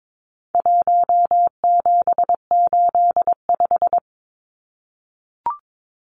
Morse code, heard as a keyed tone, about 22 wpm, 700 Hz. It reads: 1785
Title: Numbers - 4-Digit 22wpm